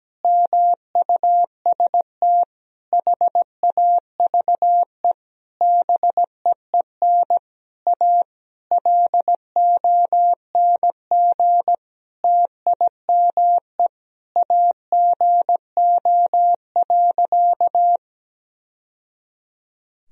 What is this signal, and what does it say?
Morse code, heard as a keyed tone, about 17 wpm, 700 Hz. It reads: MUST HAVE BEEN A LONG TIME AGO.
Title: Sentences from Top 1000 Words - Review Part 3 - 17wpm